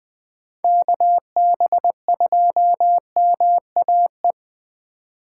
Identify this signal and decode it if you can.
Morse code, heard as a keyed tone, about 24 wpm, 700 Hz. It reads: KB2MAE
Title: Speed Racing - USA 2x3 Call Signs 20wpm